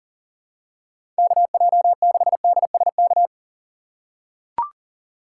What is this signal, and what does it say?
Morse code, heard as a keyed tone, about 39 wpm, 700 Hz. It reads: KJ6DSK
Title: USA Call Signs - 6 Characters - 6x Character Spacing - 40wpm